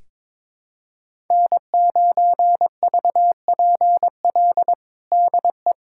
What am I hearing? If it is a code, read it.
Morse code, heard as a keyed tone, about 22 wpm, 700 Hz. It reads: N9VPL DE
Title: USA Callsign DE Callsign - 5 Characters - 22wpm